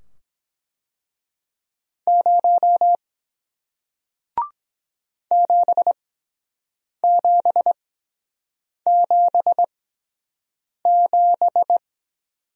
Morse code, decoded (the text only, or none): 0 7777